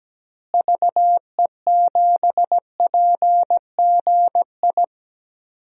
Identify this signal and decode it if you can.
Morse code, heard as a keyed tone, about 17 wpm, 700 Hz. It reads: VE7PGI
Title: DXCC-100 Call Signs - 6 Characters - Repeated 2x - 17wpm